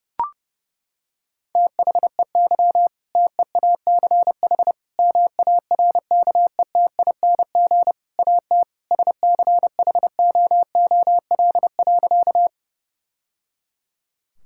Morse code, heard as a keyed tone, about 30 wpm, 700 Hz. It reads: THEY TEACH MARKETING AT SCHOOL.
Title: Sentences from Top 1000 Words - Review Part 5 - 30wpm